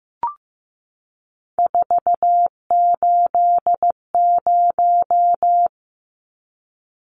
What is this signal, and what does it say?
Morse code, heard as a keyed tone, about 15 wpm, 700 Hz. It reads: 480
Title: Numbers - 3-Digit 15wpm